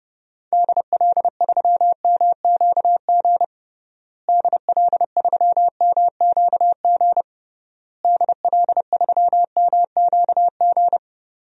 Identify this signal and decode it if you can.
Morse code, heard as a keyed tone, about 30 wpm, 700 Hz. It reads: DL3MQG DL3MQG DL3MQG
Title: DXCC-100 Call Signs - 6 Characters - Repeated 3x - 30wpm